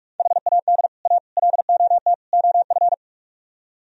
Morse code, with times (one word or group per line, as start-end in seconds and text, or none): 0.2-0.9 HAD
1.1-1.2 A
1.4-2.1 LOT
2.3-2.9 OF